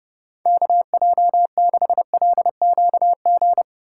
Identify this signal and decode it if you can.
Morse code, heard as a keyed tone, about 32 wpm, 700 Hz. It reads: KJ6LQG
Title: USA Call Signs - 6 Characters - 2x Character Spacing - 30wpm